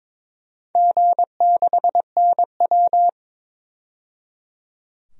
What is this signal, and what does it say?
Morse code, heard as a keyed tone, about 22 wpm, 700 Hz. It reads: G6NW
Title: DXCC-100 Call Signs - 4 Characters - 22wpm